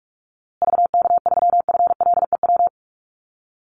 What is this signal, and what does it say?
Morse code, heard as a keyed tone, about 44 wpm, 700 Hz. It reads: VK3FLEW
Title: USA Call Signs Repeated 3x 45wpm